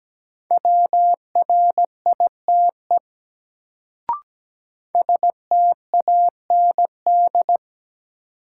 Morse code, heard as a keyed tone, about 17 wpm, 700 Hz. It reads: WRITE STAND
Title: Sentences from Top 300 Words - New Word Drill - 17wpm